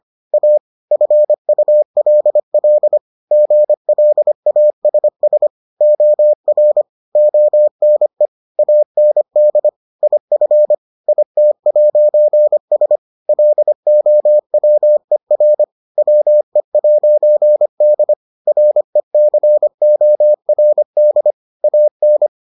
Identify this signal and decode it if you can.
Morse code, heard as a keyed tone, about 25 wpm, 600 Hz. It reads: A FULL GLASS OR ONE AND IF IT'S LOWER WE'D RECORD AN